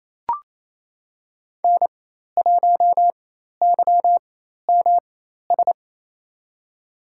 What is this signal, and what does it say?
Morse code, heard as a keyed tone, about 28 wpm, spaced as for 13 wpm, 700 Hz. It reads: N1YMS
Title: USA Call Signs - 5 Characters - 4x Character Spacing - 28wpm